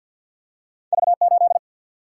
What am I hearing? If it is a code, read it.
Morse code, heard as a keyed tone, about 52 wpm, 700 Hz. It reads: V 8
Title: Single-Letter Single-Number 50wpm